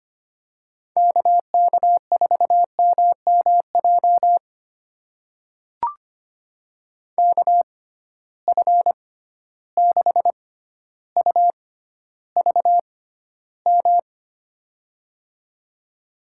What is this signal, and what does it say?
Morse code, heard as a keyed tone, about 25 wpm, spaced as for 13 wpm, 700 Hz. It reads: KK4MMJ KF6UVM
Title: USA Call Signs - 6 Characters - 6x Character Spacing - 25wpm